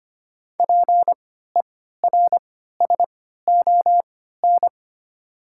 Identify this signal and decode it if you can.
Morse code, heard as a keyed tone, about 25 wpm, spaced as for 14 wpm, 700 Hz. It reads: PERSON
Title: 3x Character Spacing - Top 500 Words - 25wpm